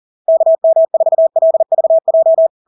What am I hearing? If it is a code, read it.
Morse code, heard as a keyed tone, about 40 wpm, 650 Hz. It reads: KM4LVJ
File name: LARK-FD-2022-Log-in-CW-40wpm-650-Hz